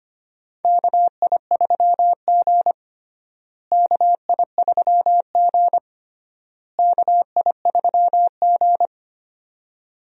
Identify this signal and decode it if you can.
Morse code, heard as a keyed tone, about 25 wpm, 700 Hz. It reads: KI3G KI3G KI3G
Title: USA Call Signs - 4 Characters - Repeated 3x - 25wpm